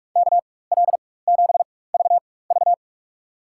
Morse code, heard as a keyed tone, about 46 wpm, spaced as for 21 wpm, 700 Hz. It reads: KL7VV
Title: USA Call Signs - 5 Characters - 4x Character Spacing - 45wpm